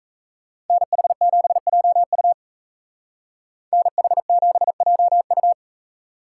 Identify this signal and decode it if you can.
Morse code, heard as a keyed tone, about 36 wpm, 700 Hz. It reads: NH7JU NH7JU